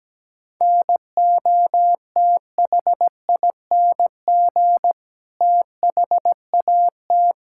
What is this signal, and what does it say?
Morse code, heard as a keyed tone, about 17 wpm, 700 Hz. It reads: NOTHING THAT
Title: Sentences from Top 1000 Words - Review Part 6 - 17wpm